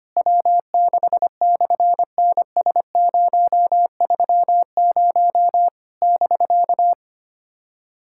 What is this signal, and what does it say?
Morse code, heard as a keyed tone, about 24 wpm, 700 Hz. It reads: W6/NS030 <BK>
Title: SOTA QSO Practice 25wpm